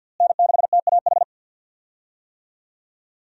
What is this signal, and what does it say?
Morse code, heard as a keyed tone, about 47 wpm, 700 Hz. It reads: N6TAH
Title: USA Call Signs - 5 Characters - Repeated 2x - 50wpm